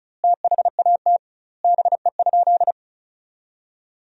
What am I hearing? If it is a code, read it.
Morse code, heard as a keyed tone, about 34 wpm, 700 Hz. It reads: THAT BE?
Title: 2x Word-Spacing - Sentences from Top 100 Words - 35wpm